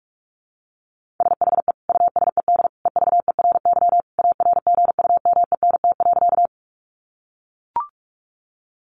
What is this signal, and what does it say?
Morse code, heard as a keyed tone, about 44 wpm, 700 Hz. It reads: SHE USED EVERY ARGUMENT.
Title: Sentences from Top 1400 Words 45wpm